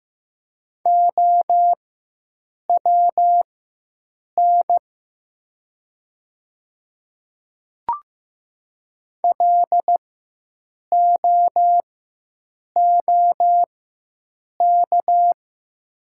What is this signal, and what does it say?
Morse code, heard as a keyed tone, about 15 wpm, spaced as for 7 wpm, 700 Hz. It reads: OWN LOOK
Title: 4x Character Spacing - Top 300 Words - 15wpm